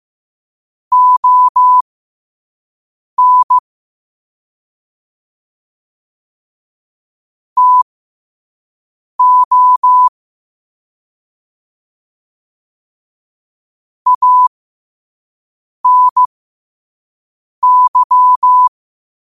Morse code, heard as a keyed tone, about 15 wpm, spaced as for 5 wpm, 1000 Hz. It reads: ON TO ANY
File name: 5wpm